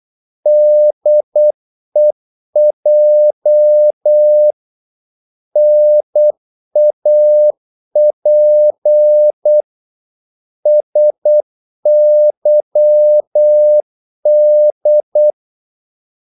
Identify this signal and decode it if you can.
Morse code, heard as a keyed tone, about 8 wpm, 600 Hz. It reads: DEJ NAP SYD
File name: Korte ord DK 8wpm